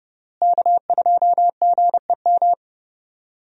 Morse code, heard as a keyed tone, about 31 wpm, 700 Hz. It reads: K2GEM
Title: USA Call Signs - 5 Characters - Repeated 2x - 30wpm